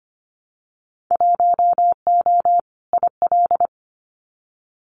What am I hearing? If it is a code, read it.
Morse code, heard as a keyed tone, about 25 wpm, 700 Hz. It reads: 1O IL
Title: Winter Field Day Exchanges 25wpm